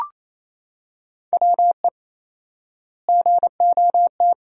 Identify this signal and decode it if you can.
Morse code, heard as a keyed tone, about 28 wpm, 700 Hz. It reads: WE GOT
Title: 4x Word-Spacing - Sentences from Top 300 Words - 28wpm